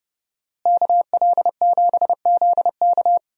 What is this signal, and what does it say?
Morse code, heard as a keyed tone, about 29 wpm, 700 Hz. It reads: KL7ZK
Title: USA Call Signs - 5 Characters - Repeated 3x - 30wpm